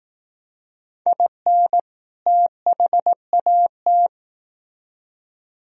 Morse code, heard as a keyed tone, about 18 wpm, 700 Hz. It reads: IN THAT
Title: Speed-Racing - Sets of 2 Words 15wpm